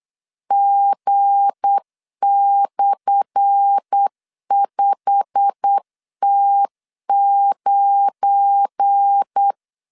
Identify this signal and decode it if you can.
Morse code, heard as a keyed tone, about 8 wpm, 800 Hz. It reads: G/5T9
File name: LEZIONI / 10 SIGNS / 10D COMA 8wpm 3749